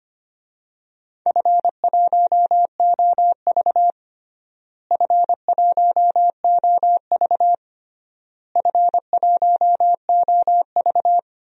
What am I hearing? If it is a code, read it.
Morse code, heard as a keyed tone, about 25 wpm, 700 Hz. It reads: F1OV F1OV F1OV
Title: DXCC-100 Call Signs - 4 Characters - Repeated 3x - 25wpm